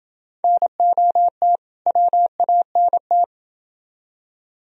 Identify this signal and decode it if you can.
Morse code, heard as a keyed tone, about 27 wpm, 700 Hz. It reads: NOT WANT